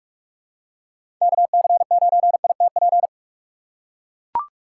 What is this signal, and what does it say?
Morse code, heard as a keyed tone, about 44 wpm, 690 Hz. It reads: KC9ITP
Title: USA Call Signs - 6 Characters - 3x Character Spacing - 45wpm